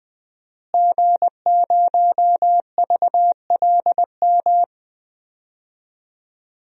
Morse code, heard as a keyed tone, about 20 wpm, 700 Hz. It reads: G0VLM
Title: DXCC-100 Call Signs - 5 Characters - Repeated 2x - 20wpm